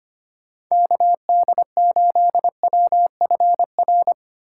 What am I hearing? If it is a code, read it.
Morse code, heard as a keyed tone, about 30 wpm, 700 Hz. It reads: KD8WFR
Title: Speed Racing - USA 2x3 Call Signs 25wpm